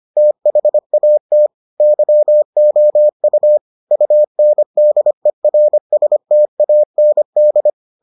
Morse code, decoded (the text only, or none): THAT YOU UNDERSTAND